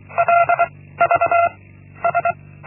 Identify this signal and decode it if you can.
Morse code, heard as a keyed tone, about 24 wpm, 690 Hz. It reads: L V S
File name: Linksmakalnis-1